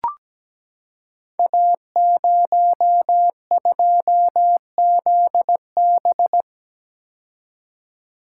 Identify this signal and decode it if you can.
Morse code, heard as a keyed tone, about 17 wpm, 700 Hz. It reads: A02ZB
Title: Code Groups - 5 Letter:Numbers - 17wpm